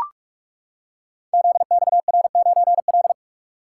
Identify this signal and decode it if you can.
Morse code, heard as a keyed tone, about 44 wpm, 700 Hz. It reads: ZXR9L
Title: Code Groups - 5 Letter:Numbers - 45wpm